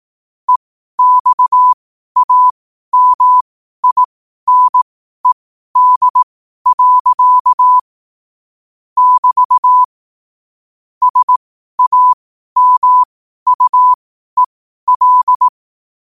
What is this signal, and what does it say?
Morse code, heard as a keyed tone, about 18 wpm, spaced as for 12 wpm, 1000 Hz. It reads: EXAMINED. <BT> SAMUEL
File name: qotdfiles / 12wpm